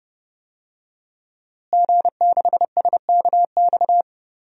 Morse code, heard as a keyed tone, about 30 wpm, 700 Hz. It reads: G6SKX
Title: DXCC-100 Call Signs - 5 Characters - Repeated 2x - 30wpm